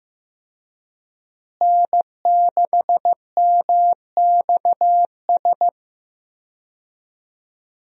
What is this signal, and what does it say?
Morse code, heard as a keyed tone, about 15 wpm, 700 Hz. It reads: N6MXS
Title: USA Call Signs - 5 Characters - Repeated 3x - 15wpm